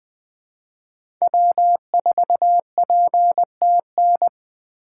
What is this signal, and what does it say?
Morse code, heard as a keyed tone, about 21 wpm, 700 Hz. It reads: W4PTN
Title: USA Call Signs - 6x Character Spacing - 20wpm